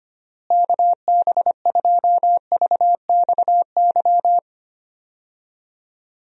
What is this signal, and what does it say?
Morse code, heard as a keyed tone, about 25 wpm, 700 Hz. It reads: KB2VXY
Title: USA Call Signs - 6 Characters - Repeated 3x - 25wpm